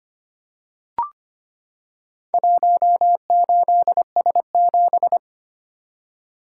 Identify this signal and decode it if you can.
Morse code, heard as a keyed tone, about 25 wpm, 700 Hz. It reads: 18S7
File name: practice / 25wpm / Four Letter-Number Combinations - 25wpm